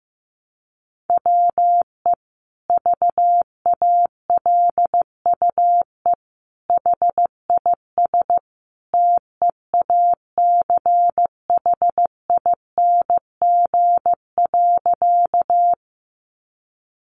Morse code, heard as a keyed tone, about 14 wpm, 700 Hz. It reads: WE VALUE HIS TEACHING.
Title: Sentences from Top 1100 Words 15wpm